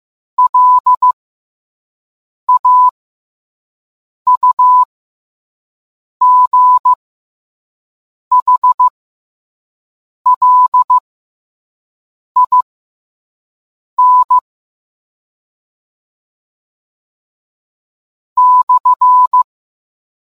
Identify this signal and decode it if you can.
Morse code, heard as a keyed tone, about 15 wpm, spaced as for 5 wpm, 1000 Hz. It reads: LAUGHLIN /